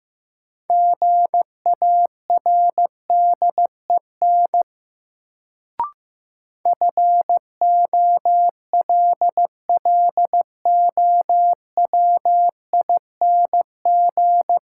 Morse code, heard as a keyed tone, about 15 wpm, 700 Hz. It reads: GARDEN FOLLOWING
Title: Sentences from Top 700 Words - New Word Drill - 15wpm